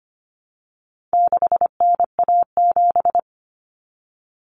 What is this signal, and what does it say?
Morse code, heard as a keyed tone, about 25 wpm, 700 Hz. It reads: 6NA7
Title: Common Cut Numbers - 4-Digit 25wpm